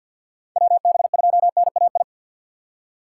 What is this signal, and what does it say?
Morse code, heard as a keyed tone, about 49 wpm, 700 Hz. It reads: WB2NAI
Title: USA Call Signs - 4x Character Spacing - 50wpm